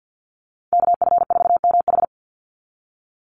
Morse code, heard as a keyed tone, about 50 wpm, 700 Hz. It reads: KF4MH
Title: NA SOTA Chaser Callsigns 50wpm